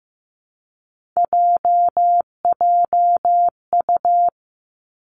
Morse code, heard as a keyed tone, about 15 wpm, 700 Hz. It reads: JJU
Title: Three Challenging Letter-Number Combinations 15wpm